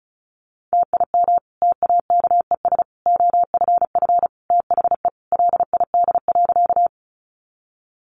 Morse code, heard as a keyed tone, about 34 wpm, 700 Hz. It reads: TIM TAKES OFF THE LID.
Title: Easy Phrases 35wpm